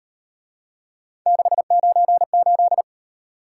Morse code, heard as a keyed tone, about 37 wpm, 700 Hz. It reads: 698